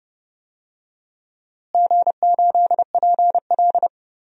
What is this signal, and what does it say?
Morse code, heard as a keyed tone, about 29 wpm, 700 Hz. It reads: G8PL
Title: DXCC-100 Call Signs - 4 Characters - Repeated 3x - 30wpm